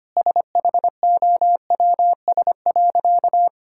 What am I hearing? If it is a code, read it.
Morse code, heard as a keyed tone, about 25 wpm, 700 Hz. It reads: SHOWS.
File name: practice / 25wpm / Difficult Sentences with Hints - 25wpm